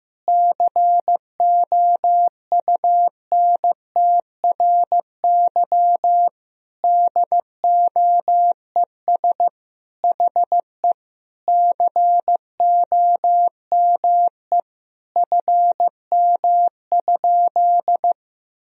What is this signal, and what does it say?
Morse code, heard as a keyed tone, about 15 wpm, 700 Hz. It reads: COUNTRY DOES HE COME FM?
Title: Sentences from Top 300 Words and Abbreviations 15wpm